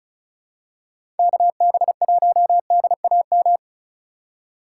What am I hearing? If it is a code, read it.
Morse code, heard as a keyed tone, about 34 wpm, 700 Hz. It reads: KB1DAM
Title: USA Call Signs - 6 Characters - 4x Character Spacing - 35wpm